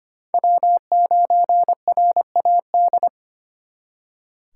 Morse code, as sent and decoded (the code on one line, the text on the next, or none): .-- ----. .-. .- -..
W9RAD